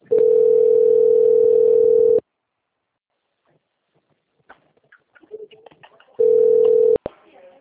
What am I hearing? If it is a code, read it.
Ringback tone (North America), 440 + 480 Hz.